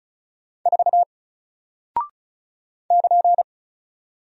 Morse code, heard as a keyed tone, about 36 wpm, 700 Hz. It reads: 4 <KN>
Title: Single Character (Letter:Number:Punctuation:Prosign) Alt 35wpm